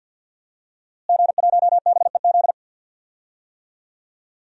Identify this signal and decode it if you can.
Morse code, heard as a keyed tone, about 53 wpm, 700 Hz. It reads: G1BEZ
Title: DXCC-100 Call Signs - 5 Characters - Repeated 2x - 50wpm